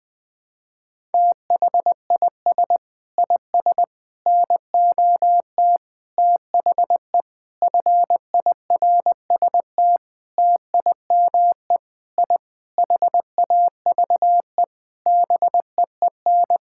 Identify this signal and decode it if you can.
Morse code, heard as a keyed tone, about 20 wpm, 700 Hz. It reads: THIS IS NOT THE FIRST TIME I HAVE BEEN
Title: Sentences with 6-Word Phrases - Spoken Beginning - 20wpm